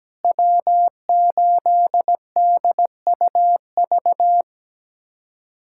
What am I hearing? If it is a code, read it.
Morse code, heard as a keyed tone, about 17 wpm, 700 Hz. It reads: W8DUV
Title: USA Callsign DE Callsign 17wpm